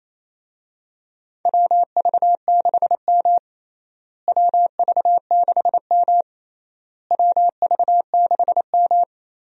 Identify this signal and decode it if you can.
Morse code, heard as a keyed tone, about 28 wpm, 700 Hz. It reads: WV6M WV6M WV6M
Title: USA Call Signs - 4 Characters - Repeated 3x - 28wpm